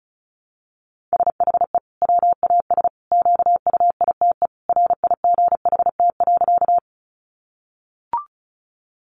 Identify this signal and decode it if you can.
Morse code, heard as a keyed tone, about 34 wpm, 700 Hz. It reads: SHE WAS QUITE RIGHT.
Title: Sentences from Top 700 Words 35wpm